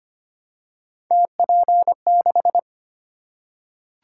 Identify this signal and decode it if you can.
Morse code, heard as a keyed tone, about 25 wpm, 700 Hz. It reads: TP6
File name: Three Letter-Number Combinations - 25wpm